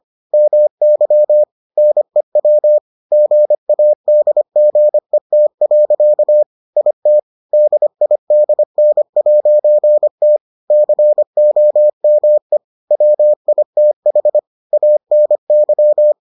Morse code, heard as a keyed tone, about 25 wpm, 600 Hz. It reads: MY NEW GADGET. IT DIDN'T COME WITH ANY